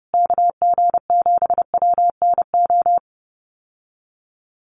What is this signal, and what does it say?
Morse code, heard as a keyed tone, about 30 wpm, 700 Hz. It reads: KG7WNO
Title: USA Call Signs Repeated 2x 30wpm